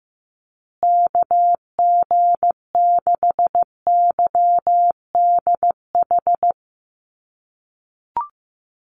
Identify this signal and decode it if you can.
Morse code, heard as a keyed tone, about 15 wpm, 700 Hz. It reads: KG6YDH
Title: USA Call Signs Repeated 3x 15wpm